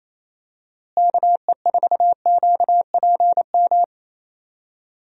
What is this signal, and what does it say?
Morse code, heard as a keyed tone, about 28 wpm, 700 Hz. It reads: KE4QPM
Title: USA Call Signs - 6 Characters - Repeated 3x - 28wpm